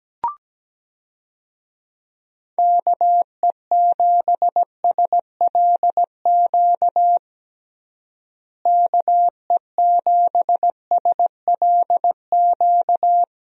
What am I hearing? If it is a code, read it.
Morse code, heard as a keyed tone, about 17 wpm, 700 Hz. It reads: KE7SLQ KE7SLQ
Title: USA Call Signs - 6 Characters - Repeated 3x - 17wpm